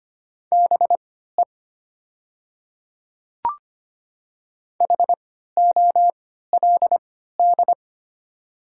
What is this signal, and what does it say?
Morse code, heard as a keyed tone, about 25 wpm, spaced as for 14 wpm, 700 Hz. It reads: BE HOLD
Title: 3x Character Spacing - Top 500 Words - 25wpm